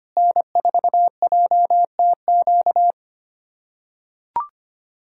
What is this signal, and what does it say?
Morse code, heard as a keyed tone, about 25 wpm, 700 Hz. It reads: N4JTQ
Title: USA Callsign DE Callsign - 5 Characters - 25wpm